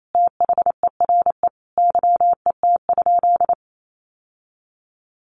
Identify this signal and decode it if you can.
Morse code, heard as a keyed tone, about 28 wpm, 700 Hz. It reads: THERE YET?
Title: Sentences from Top 1300 Words 28wpm